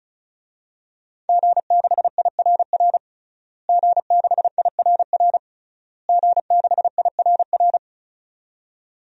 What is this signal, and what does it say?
Morse code, heard as a keyed tone, about 36 wpm, 700 Hz. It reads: G6IRR G6IRR G6IRR
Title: DXCC-100 Call Signs - 5 Characters - Repeated 3x - 35wpm